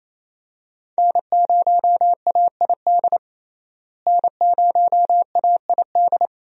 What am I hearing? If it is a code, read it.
Morse code, heard as a keyed tone, about 28 wpm, 700 Hz. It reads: N0AID N0AID
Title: USA Call Signs - 5 Characters - Repeated 2x - 28wpm